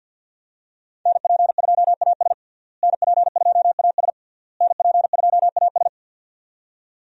Morse code, heard as a keyed tone, about 46 wpm, 700 Hz. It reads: NP2AS NP2AS NP2AS
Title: USA Call Signs - 5 Characters - Repeated 3x - 50wpm